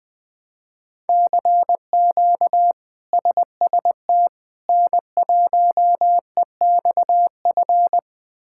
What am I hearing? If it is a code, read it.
Morse code, heard as a keyed tone, about 20 wpm, 700 Hz. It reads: CQ SST N1EXF
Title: K1USN SST QSO Practice - 20wpm